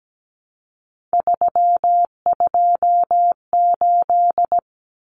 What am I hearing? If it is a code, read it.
Morse code, heard as a keyed tone, about 17 wpm, 700 Hz. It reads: 328
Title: Common Cut Numbers - 3-Digit 17wpm